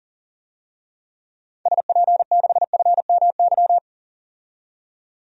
Morse code, heard as a keyed tone, about 40 wpm, 700 Hz. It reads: SP6FMY
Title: DXCC-100 Call Signs - 6 Characters - 40wpm